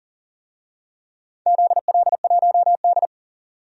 Morse code, heard as a keyed tone, about 38 wpm, 700 Hz. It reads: ZL1D
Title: DXCC-100 Call Signs - 4 Characters - 40wpm